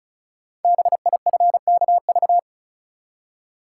Morse code, heard as a keyed tone, about 36 wpm, 690 Hz. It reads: BIFKV
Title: Code Groups - 5 Letter:Numbers - 35wpm